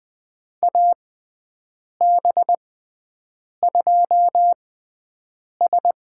Morse code, heard as a keyed tone, about 20 wpm, spaced as for 7 wpm, 700 Hz. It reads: AB2S
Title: USA Call Signs - 6x Character Spacing - 20wpm